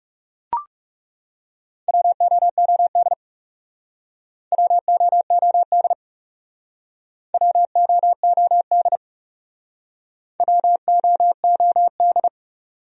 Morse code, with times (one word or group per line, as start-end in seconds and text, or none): 1.9-3.1 WOOD
4.5-5.9 WOOD
7.3-9.0 WOOD
10.4-12.3 WOOD